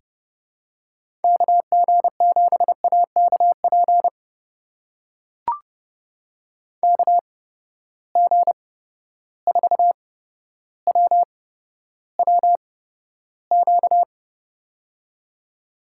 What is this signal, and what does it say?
Morse code, heard as a keyed tone, about 32 wpm, spaced as for 13 wpm, 700 Hz. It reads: KG7AKP KG4WWQ
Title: USA Call Signs - 6 Characters - 8x Character Spacing - 30wpm